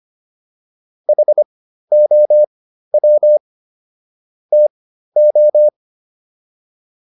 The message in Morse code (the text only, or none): HOW TO